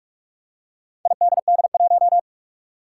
Morse code, heard as a keyed tone, about 44 wpm, 700 Hz. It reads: IDD1
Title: Four Challenging Letter-Number Combinations 45wpm